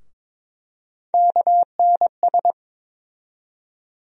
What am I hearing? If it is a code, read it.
Morse code, heard as a keyed tone, about 22 wpm, 700 Hz. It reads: KNS
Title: USA Call Sign Postfixes - 3 Letters 22wpm